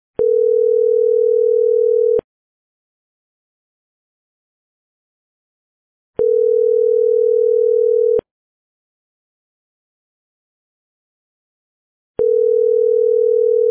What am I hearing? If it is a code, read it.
Ringback tone (North America), 440 + 480 Hz.